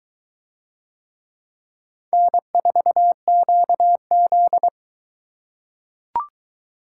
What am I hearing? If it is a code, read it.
Morse code, heard as a keyed tone, about 21 wpm, 700 Hz. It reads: N4QZ